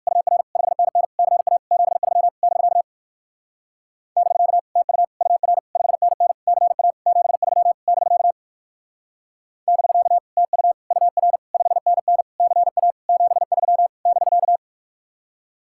Morse code, heard as a keyed tone, about 54 wpm, 700 Hz. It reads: UR 5NN CA 73 <BK> <BK> TU UR 5NN CA 73 <BK> <BK> TU UR 5NN CA 73 <BK>
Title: Speed-Racing Common QSO Phrases - 45wpm